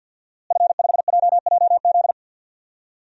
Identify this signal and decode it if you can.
Morse code, heard as a keyed tone, about 51 wpm, 700 Hz. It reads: F5JJZ
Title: DXCC-100 Call Signs - 5 Characters - Repeated 3x - 50wpm